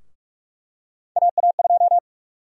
Morse code, heard as a keyed tone, about 44 wpm, 700 Hz. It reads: AA2
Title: USA Call Sign Prefixes - 45wpm